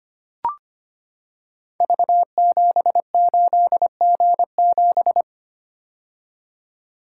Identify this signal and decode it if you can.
Morse code, heard as a keyed tone, about 25 wpm, 700 Hz. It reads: V78G7
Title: Code Groups - 5 Letter:Numbers - 25wpm